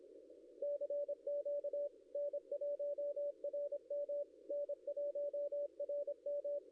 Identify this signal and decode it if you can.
Morse code, heard as a keyed tone, about 25 wpm, 560 Hz, with radio noise. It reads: CQ N1RM N1RM
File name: contests / ss-cw-k5tr-op-n5rz -0700